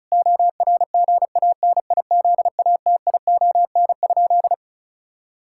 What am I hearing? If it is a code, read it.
Morse code, heard as a keyed tone, about 34 wpm, 700 Hz. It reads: ORGANIZATION?
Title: Sentences from Top 1000 Words - Review Part 5 - 35wpm